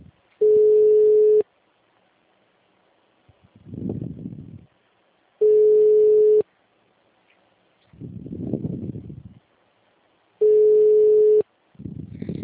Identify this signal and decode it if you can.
Ringback tone (Europe, 425 Hz).
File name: out- -2007- - - .208